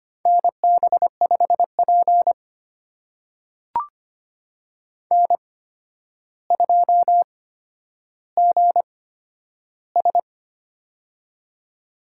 Morse code, heard as a keyed tone, about 25 wpm, spaced as for 11 wpm, 700 Hz. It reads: NB5P N2GS
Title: USA Call Signs - 4 Characters - 8x Character Spacing - 25wpm